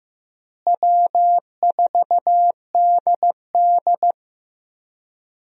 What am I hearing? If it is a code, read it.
Morse code, heard as a keyed tone, about 15 wpm, 700 Hz. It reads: W4DD
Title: USA Call Signs - 4 Characters - Repeated 3x - 15wpm